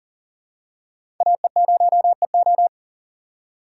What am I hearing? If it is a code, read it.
Morse code, heard as a keyed tone, about 40 wpm, 700 Hz. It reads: AE0EO
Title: USA Call Signs - 5 Characters - 3x Character Spacing - 40wpm